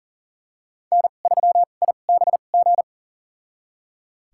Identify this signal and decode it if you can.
Morse code, heard as a keyed tone, about 40 wpm, spaced as for 29 wpm, 700 Hz. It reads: N3IBG
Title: USA Call Signs - 2x Character Spacing - 40wpm